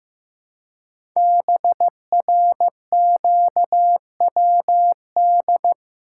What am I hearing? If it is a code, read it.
Morse code, heard as a keyed tone, about 15 wpm, 700 Hz. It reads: BRQWD
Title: Code Groups - 5 Letters - 15wpm